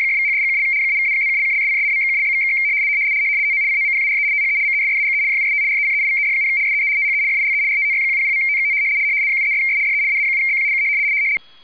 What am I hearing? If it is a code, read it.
RTTY, about 100 baud. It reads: !=#$&() -=-+()()':',./()?=// TO COMPARE THE MESSAGE JUST RECEIVED, SEE THE FILE RTTY100.TXT 73'S DE AA7BQ K